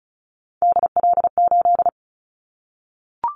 Morse code, heard as a keyed tone, about 36 wpm, 700 Hz. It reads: DL8
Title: Callsign Prefixes for Top 50 DXCC - 35wpm